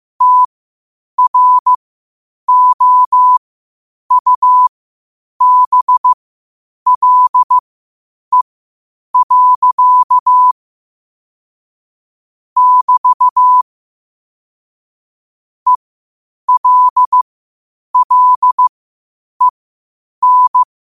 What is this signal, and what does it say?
Morse code, heard as a keyed tone, about 15 wpm, spaced as for 8 wpm, 1000 Hz. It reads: TROUBLE. <BT> ELLEN
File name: qotdfiles / 8wpm